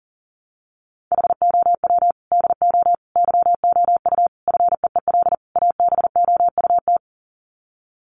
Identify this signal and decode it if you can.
Morse code, heard as a keyed tone, about 40 wpm, 700 Hz. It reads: HOW DO YOU FEEL ABOUT
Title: Sets of 5 Words 40wpm